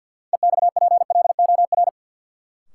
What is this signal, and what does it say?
Morse code, heard as a keyed tone, about 47 wpm, 700 Hz. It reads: EXPLOR
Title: Sentences from Top 1100 Words - New Word Drill Alt - 50wpm